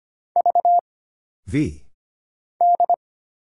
Morse code, heard as a keyed tone, about 25 wpm, 700 Hz. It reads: V D
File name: Single Character (Letter:Number:Punctuation:Prosign) - Mind-Melt - 25wpm